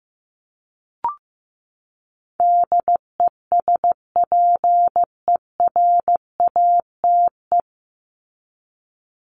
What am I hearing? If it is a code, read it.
Morse code, heard as a keyed tone, about 15 wpm, 700 Hz. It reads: DESPERATE
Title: Top 1751-2000 Words 15wpm